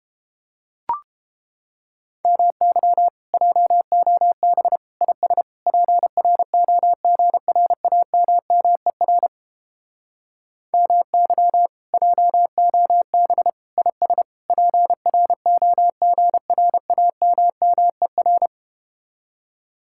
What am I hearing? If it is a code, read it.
Morse code, heard as a keyed tone, about 32 wpm, 700 Hz. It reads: MY JOB IS PROGRAMMER MY JOB IS PROGRAMMER